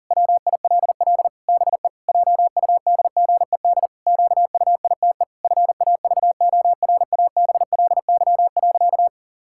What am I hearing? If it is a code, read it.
Morse code, heard as a keyed tone, about 40 wpm, 700 Hz. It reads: WILL BE JUDGED QUITE FAVORABLY.
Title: Pangram Challenge v2 - 40wpm